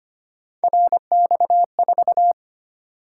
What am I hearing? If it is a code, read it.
Morse code, heard as a keyed tone, about 25 wpm, 700 Hz. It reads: RX4
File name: Three Letter-Number Combinations - 25wpm